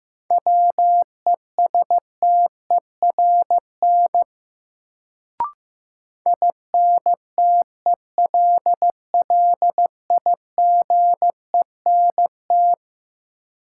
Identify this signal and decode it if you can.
Morse code, heard as a keyed tone, about 15 wpm, 700 Hz. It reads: WESTERN INTELLIGENT
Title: Sentences from Top 1000 Words - Review Part 7 - New Word Drill - 15wpm